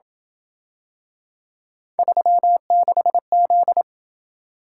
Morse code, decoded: 36Z

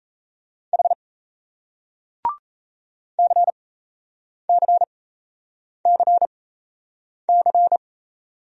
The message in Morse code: H CCCC

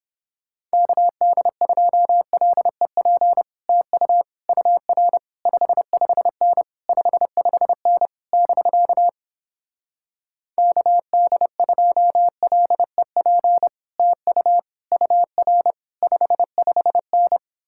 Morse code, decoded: KD2LEP TU UR 55N 55N <BK> KD2LEP TU UR 55N

